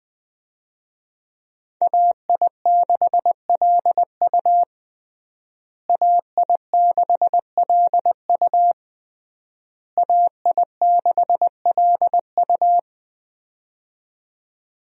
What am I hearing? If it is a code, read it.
Morse code, heard as a keyed tone, about 20 wpm, 700 Hz. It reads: AI6LU AI6LU AI6LU
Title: USA Call Signs - 5 Characters - Repeated 3x - 20wpm